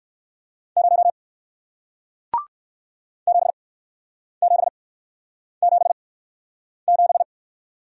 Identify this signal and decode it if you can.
Morse code, heard as a keyed tone, about 55 wpm, spaced as for 9 wpm, 700 Hz. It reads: 0 7777
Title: Speed-Racing Single Letter-Number 45wpm